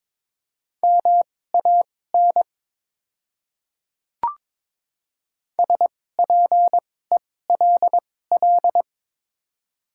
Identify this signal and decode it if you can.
Morse code, heard as a keyed tone, about 22 wpm, spaced as for 16 wpm, 700 Hz. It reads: MAN SPELL
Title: 2x Character Spacing - Top 300 Words - 22wpm